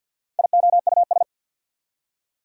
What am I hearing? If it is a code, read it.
Morse code, heard as a keyed tone, about 53 wpm, 690 Hz. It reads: IOUS
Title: Prefixes and Suffixes 50wpm